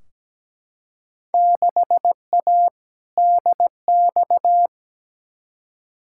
Morse code, heard as a keyed tone, about 17 wpm, 700 Hz. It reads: 6A DX
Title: Field Day Exchanges Encore - 17wpm